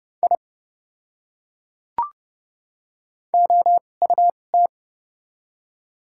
Morse code, heard as a keyed tone, about 30 wpm, spaced as for 22 wpm, 700 Hz. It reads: I OUT
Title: 2x Character Spacing - Top 200 Words - 30wpm